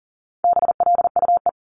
Morse code, heard as a keyed tone, about 39 wpm, 700 Hz. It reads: BLUE
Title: Multi-letter Phonemes 40wpm